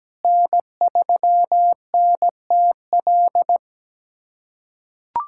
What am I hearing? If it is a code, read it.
Morse code, heard as a keyed tone, about 17 wpm, 700 Hz. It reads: N3NTL
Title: USA Callsign DE Callsign - 5 Characters - 17wpm